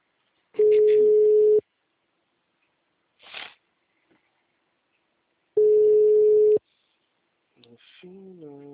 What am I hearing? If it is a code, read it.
Ringback tone (Europe, 425 Hz).
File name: out- -2008- - - .2